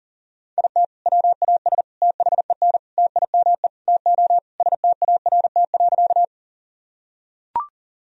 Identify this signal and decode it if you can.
Morse code, heard as a keyed tone, about 40 wpm, 700 Hz. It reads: IT WAS THEN TIME TO START.
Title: Sentences from Top 1000 Words - Review Part 1 - 40wpm